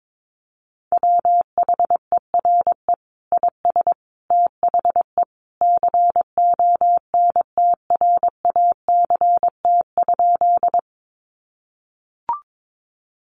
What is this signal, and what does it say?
Morse code, heard as a keyed tone, about 22 wpm, 700 Hz. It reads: WHERE IS THE CONTRACT?
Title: Sentences from Top 1400 Words 22wpm